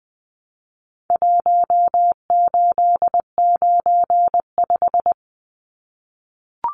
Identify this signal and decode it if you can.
Morse code, heard as a keyed tone, about 20 wpm, 700 Hz. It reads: 1895
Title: Numbers - Contest Style - 4-Digit 20wpm